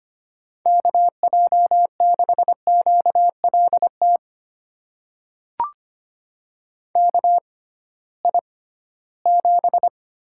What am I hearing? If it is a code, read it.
Morse code, heard as a keyed tone, about 25 wpm, spaced as for 16 wpm, 700 Hz. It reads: KJ6QLT KI7
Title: USA Call Signs - 6 Characters - 6x Character Spacing - 25wpm